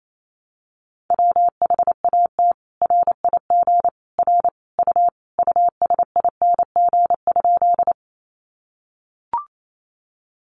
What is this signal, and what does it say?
Morse code, heard as a keyed tone, about 28 wpm, 700 Hz. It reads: WHAT RIG R U USING?
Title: Common QSO Phrases 28wpm